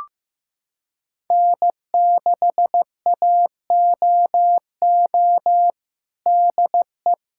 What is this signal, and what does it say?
Morse code, heard as a keyed tone, about 16 wpm, 700 Hz. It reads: N6AOO DE
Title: USA Callsign DE Callsign - 5 Characters - 15wpm